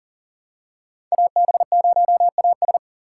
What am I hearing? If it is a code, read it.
Morse code, heard as a keyed tone, about 40 wpm, 700 Hz. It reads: AB0AS
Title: USA Call Signs - 3x Character Spacing - 40wpm